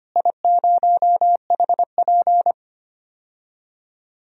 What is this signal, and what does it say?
Morse code, heard as a keyed tone, about 25 wpm, 700 Hz. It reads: I0HP
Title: DXCC-100 Call Signs - 4 Characters - Repeated 3x - 25wpm